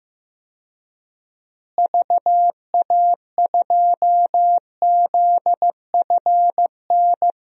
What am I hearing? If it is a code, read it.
Morse code, heard as a keyed tone, about 15 wpm, 700 Hz. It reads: VA2ZFN
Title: DXCC-100 Call Signs - 6 Characters - Repeated 3x - 15wpm